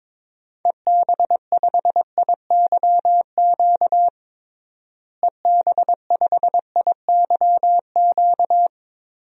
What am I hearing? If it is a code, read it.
Morse code, heard as a keyed tone, about 22 wpm, 700 Hz. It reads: EB5IYQ EB5IYQ
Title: DXCC-100 Call Signs - 6 Characters - Repeated 2x - 22wpm